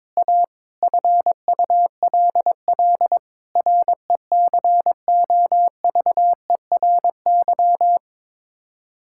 Morse code, heard as a keyed tone, about 22 wpm, 700 Hz. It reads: A FULL RECOVERY
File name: practice / 22wpm / Sets of 7 Words - 22wpm